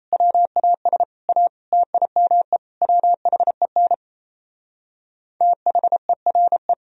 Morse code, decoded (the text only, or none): WAS A TIME WHEN THERE